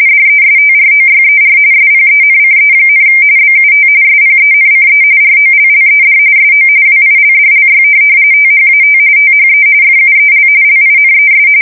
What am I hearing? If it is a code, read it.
RTTY, about 45.45 baud. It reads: DESTROYER SPENCE, BUT SPENCE ROLLED SO WILDLY THAT THE FUEL LINES
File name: fsk170 / USS-New-Jersey